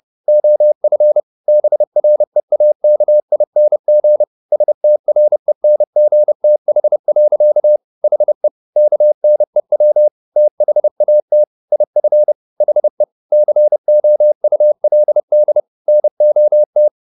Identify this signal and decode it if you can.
Morse code, heard as a keyed tone, about 30 wpm, 600 Hz. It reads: OF BREAKING STRENGTH. HE KNEW THAT IF HE COULD NOT
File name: the old man and the sea / the old man and the sea 1 30wpm